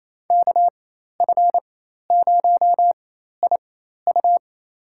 Morse code, heard as a keyed tone, about 28 wpm, spaced as for 13 wpm, 700 Hz. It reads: KF0IU
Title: USA Call Signs - 5 Characters - 4x Character Spacing - 28wpm